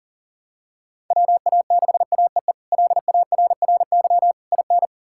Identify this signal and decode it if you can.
Morse code, heard as a keyed tone, about 37 wpm, 700 Hz. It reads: WA6AEE LARRY IN
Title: K1USN SST QSO Practice - 40wpm